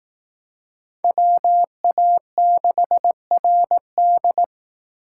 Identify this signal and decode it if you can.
Morse code, heard as a keyed tone, about 18 wpm, 700 Hz. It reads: WA6RD
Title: Speed Racing - USA 2x2 Call Signs 15wpm